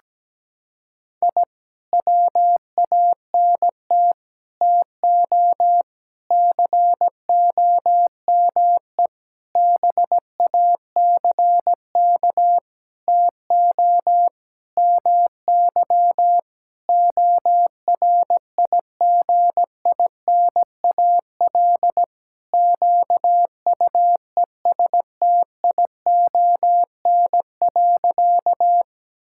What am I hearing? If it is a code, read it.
Morse code, heard as a keyed tone, about 17 wpm, 700 Hz. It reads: I WANT TO COME BACK TO MY ORIGINAL QUESTION.
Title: Sentences from Top 1000 Words - Review Part 6 - 17wpm